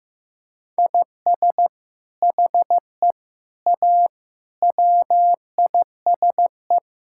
Morse code, heard as a keyed tone, about 16 wpm, 700 Hz. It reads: IS HE A WISE
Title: Sentences from Top 1600 Words 15wpm